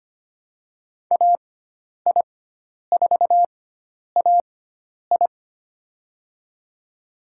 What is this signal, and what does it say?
Morse code, heard as a keyed tone, about 25 wpm, spaced as for 10 wpm, 700 Hz. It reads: AI4AI
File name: Lesson 16g - Letter c - Call Signs 25wpm